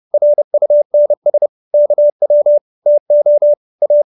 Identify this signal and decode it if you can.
Morse code, heard as a keyed tone, about 30 wpm, 590 Hz. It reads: RUNS KW TO A
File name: QSO316 30